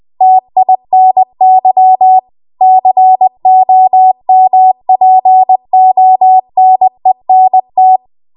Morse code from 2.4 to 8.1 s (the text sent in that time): COMPONENT